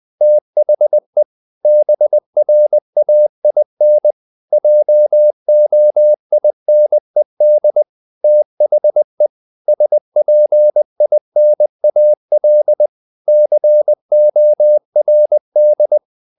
Morse code, heard as a keyed tone, about 20 wpm, 600 Hz. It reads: THE BRAIN JOINED THE SPINAL CORD
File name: the old man and the sea part 2 20wpm